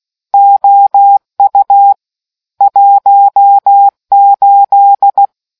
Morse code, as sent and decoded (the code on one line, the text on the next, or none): --- ..- / .---- ---..
OU 18